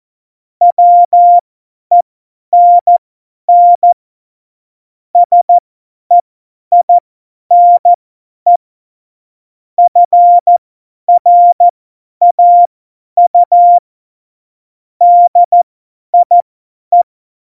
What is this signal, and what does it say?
Morse code, heard as a keyed tone, about 14 wpm, spaced as for 10 wpm, 700 Hz. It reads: WENN SEINE FRAU DIE